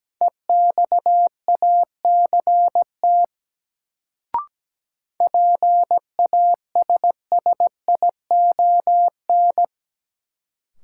Morse code, heard as a keyed tone, about 17 wpm, 700 Hz. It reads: EXACT PASSION
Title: Sentences from Top 1300 Words - New Word Drill - 17wpm